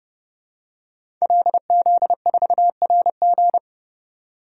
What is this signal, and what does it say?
Morse code, heard as a keyed tone, about 30 wpm, 700 Hz. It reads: LZ4RG
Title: DXCC-100 Call Signs - 5 Characters - 30wpm